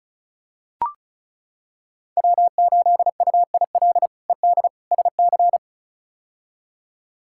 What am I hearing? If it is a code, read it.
Morse code, heard as a keyed tone, about 36 wpm, 700 Hz. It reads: W8UIL ED SC
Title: K1USN SST QSO Practice - 35wpm